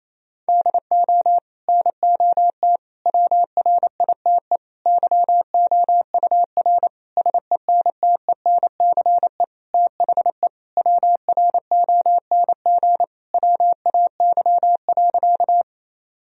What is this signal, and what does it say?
Morse code, heard as a keyed tone, about 28 wpm, 700 Hz. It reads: DO NOT WRITE YOUR SENTENCE THE WRONG WAY.
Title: Sentences from Top 1000 Words - Review Part 1 - 28wpm